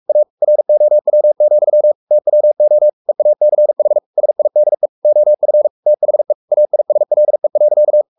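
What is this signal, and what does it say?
Morse code, heard as a keyed tone, about 44 wpm, 600 Hz. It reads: A ROW, TWO EACH SIDE OF THE AISLE.
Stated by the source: Morse code at 43 wpm sending Ken Follett's novel Never in English, from CW med Gnister.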